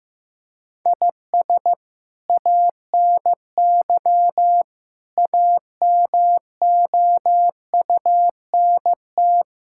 Morse code, heard as a keyed tone, about 15 wpm, 700 Hz. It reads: IS ANY AMOUNT
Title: Sentences from Top 1000 Words - Review Part 6 - 15wpm